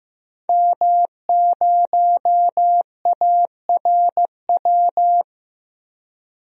Morse code, heard as a keyed tone, about 15 wpm, 700 Hz. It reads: M0ARW
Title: DXCC-100 Call Signs - 5 Characters - 15wpm